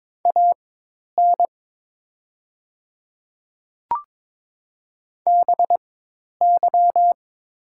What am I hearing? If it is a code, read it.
Morse code, heard as a keyed tone, about 22 wpm, spaced as for 10 wpm, 700 Hz. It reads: AN BY
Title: 4x Character Spacing - Top 300 Words - 22wpm